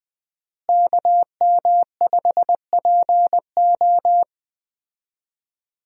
Morse code, heard as a keyed tone, about 20 wpm, 700 Hz. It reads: KM5PO
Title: USA Call Signs - 5 Characters - Repeated 2x - 20wpm